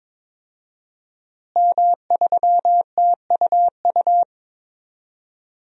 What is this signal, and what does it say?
Morse code, heard as a keyed tone, about 22 wpm, 700 Hz. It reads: M3TUU
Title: DXCC-100 Call Signs - 5 Characters - Repeated 2x - 22wpm